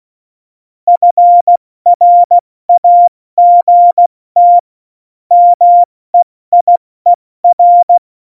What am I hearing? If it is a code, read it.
Morse code, heard as a keyed tone, about 16 wpm, spaced as for 14 wpm, 700 Hz. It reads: FRAGT MEIER
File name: Morsekurs / Witze / Witz 19. September 17, Tempo 70